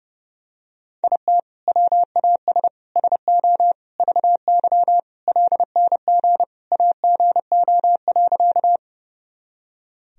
Morse code, heard as a keyed tone, about 30 wpm, 700 Hz. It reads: IT WAS SO VY LNG AGO.
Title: Sentences from Top 500 Words and Abbreviations 30wpm